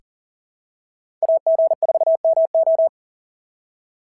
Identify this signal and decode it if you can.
Morse code, heard as a keyed tone, about 40 wpm, 650 Hz. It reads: AG4MO
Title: SOTA QSO Practice 40wpm